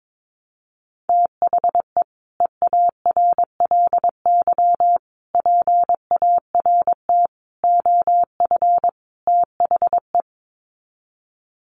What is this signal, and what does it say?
Morse code, heard as a keyed tone, about 22 wpm, 700 Hz. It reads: THE EARLY PART OF THE
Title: Sets of 5 Words Encore2 22wpm